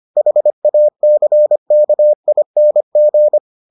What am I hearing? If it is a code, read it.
Morse code, heard as a keyed tone, about 25 wpm, 600 Hz. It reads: HACKING